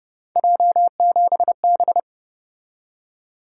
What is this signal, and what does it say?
Morse code, heard as a keyed tone, about 30 wpm, 700 Hz. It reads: J7B
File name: Speed-Racing - Three Letter-Number Combinations - 25wpm